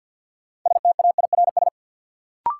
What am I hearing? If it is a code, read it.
Morse code, heard as a keyed tone, about 51 wpm, 700 Hz. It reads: STAIRS
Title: Sets of 5 Words Encore - New Word Drill - 50wpm